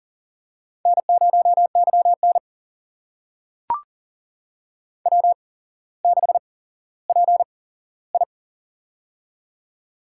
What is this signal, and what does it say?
Morse code, heard as a keyed tone, about 40 wpm, spaced as for 17 wpm, 700 Hz. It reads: N0YN W6PI
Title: USA Call Signs - 4 Characters - 8x Character Spacing - 40wpm